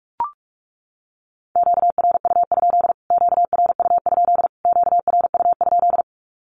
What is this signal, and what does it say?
Morse code, heard as a keyed tone, about 44 wpm, 700 Hz. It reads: QRU? QRU? QRU?
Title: Q-Codes Repeated 3x 45wpm